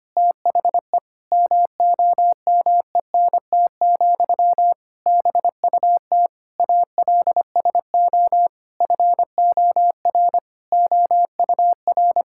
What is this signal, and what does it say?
Morse code, heard as a keyed tone, about 25 wpm, 700 Hz. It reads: THE MOMENT, BUT ALSO FOR OUR
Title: Sentences from Top 1000 Words - Review Part 6 - 25wpm